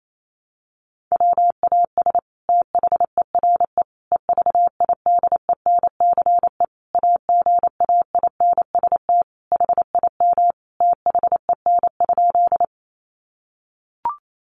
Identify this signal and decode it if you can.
Morse code, heard as a keyed tone, about 28 wpm, 700 Hz. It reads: WAS THERE EVIDENCE AGAINST HIM THEN?
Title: Sentences from Top 1300 Words 28wpm